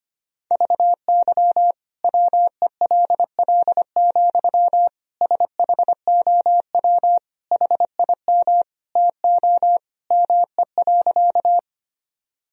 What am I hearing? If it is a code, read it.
Morse code, heard as a keyed tone, about 25 wpm, 700 Hz. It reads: VY WELL, SHOW HIM TO ME.
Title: Sentences from Top 200 Words and Abbreviations 25wpm